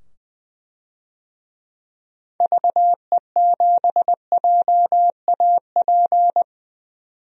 Morse code, heard as a keyed tone, about 20 wpm, 700 Hz. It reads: VE7JAP
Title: DXCC-100 Call Signs - 6 Characters - 20wpm